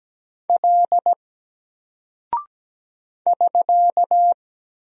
Morse code, heard as a keyed tone, about 17 wpm, 700 Hz. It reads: L <SK>
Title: Single Character (Letter:Number:Punctuation:Prosign) 17wpm